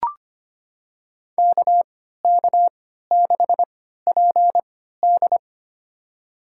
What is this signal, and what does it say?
Morse code, heard as a keyed tone, about 25 wpm, spaced as for 14 wpm, 700 Hz. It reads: KK6PD
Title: USA Call Signs - 5 Characters - 3x Character Spacing - 25wpm